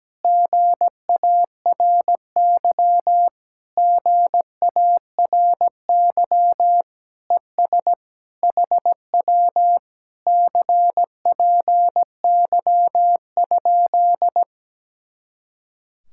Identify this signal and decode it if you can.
Morse code, heard as a keyed tone, about 17 wpm, 700 Hz. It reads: GARY GARY ES HW CPY?
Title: Speed-Racing Common QSO Phrases - 17wpm